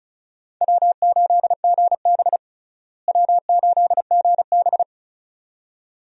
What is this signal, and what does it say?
Morse code, heard as a keyed tone, about 36 wpm, 700 Hz. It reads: W8GB W8GB
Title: USA Call Signs - 4 Characters - Repeated 2x - 35wpm